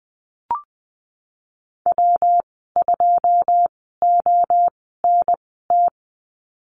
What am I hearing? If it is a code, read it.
Morse code, heard as a keyed tone, about 20 wpm, spaced as for 15 wpm, 700 Hz. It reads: W2ONT
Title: 128 Lesson 21g - Number 6 - Call Signs 20wpm